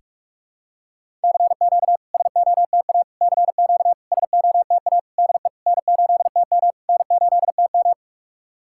Morse code, heard as a keyed tone, about 46 wpm, 690 Hz. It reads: CQ SOTA CQ SOTA DE N8TM N8TM
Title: SOTA QSO Practice 45wpm